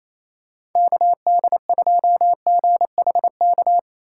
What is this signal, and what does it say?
Morse code, heard as a keyed tone, about 28 wpm, 700 Hz. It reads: KD2GHK
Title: USA Call Signs - 6 Characters - 3x Character Spacing - 28wpm